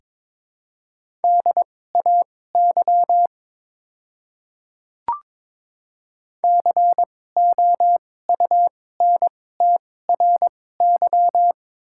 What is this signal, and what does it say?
Morse code, heard as a keyed tone, about 22 wpm, spaced as for 16 wpm, 700 Hz. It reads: DAY COUNTRY
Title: 2x Character Spacing - Top 300 Words - 22wpm